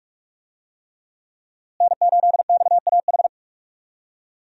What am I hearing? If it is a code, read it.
Morse code, heard as a keyed tone, about 46 wpm, 700 Hz. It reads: N8XAH
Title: USA Call Signs - 5 Characters - Repeated 3x - 45wpm